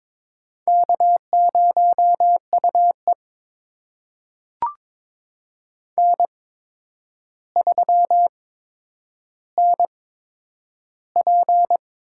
Morse code, heard as a keyed tone, about 22 wpm, spaced as for 9 wpm, 700 Hz. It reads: K0UE N3NP
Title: USA Call Signs - 4 Characters - 8x Character Spacing - 22wpm